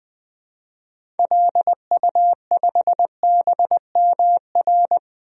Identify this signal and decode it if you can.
Morse code, heard as a keyed tone, about 20 wpm, 700 Hz. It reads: LU5BMR
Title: DXCC-100 Call Signs - 6 Characters - 20wpm